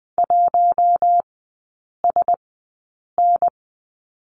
Morse code, heard as a keyed tone, about 20 wpm, spaced as for 8 wpm, 700 Hz. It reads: 1SN
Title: 053 Lesson 10e - Letter r - Three Characters 20wpm